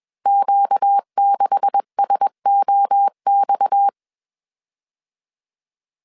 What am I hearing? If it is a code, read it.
Morse code, heard as a keyed tone, about 21 wpm, 800 Hz. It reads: Q6SOX
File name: CW / DET / 1 9 1500C S C 20wpm 2310